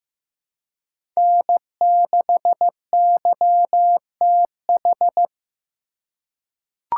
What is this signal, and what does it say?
Morse code, heard as a keyed tone, about 15 wpm, 700 Hz. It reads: N6YTH
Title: USA Call Signs - 5 Characters - 4x Character Spacing - 15wpm